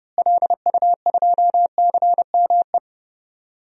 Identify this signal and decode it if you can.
Morse code, heard as a keyed tone, about 31 wpm, 700 Hz. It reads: LU2CME
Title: DXCC-100 Call Signs - 6 Characters - Repeated 3x - 30wpm